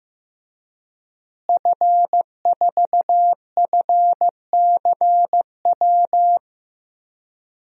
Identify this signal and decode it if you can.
Morse code, heard as a keyed tone, about 15 wpm, 700 Hz. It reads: F4FCW
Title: DXCC-100 Call Signs - 5 Characters - Repeated 3x - 15wpm